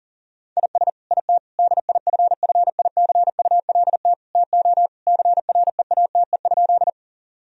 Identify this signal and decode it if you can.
Morse code, heard as a keyed tone, about 40 wpm, 700 Hz. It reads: IS IT DIFFICULT TO CREATE?
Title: Sentences from Top 1000 Words - Review Part 3 - 40wpm